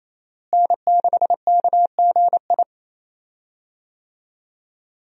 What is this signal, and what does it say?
Morse code, heard as a keyed tone, about 28 wpm, 700 Hz. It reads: N6KGI
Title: USA Call Signs - 5 Characters - Repeated 2x - 28wpm